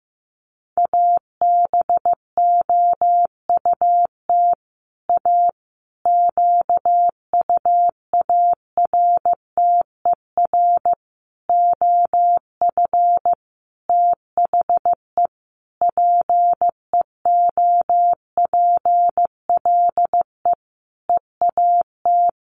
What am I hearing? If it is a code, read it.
Morse code, heard as a keyed tone, about 14 wpm, 700 Hz. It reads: ABOUT A QUARTER OF THE PEOPLE EAT
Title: Sentences from Top 1300 Words 15wpm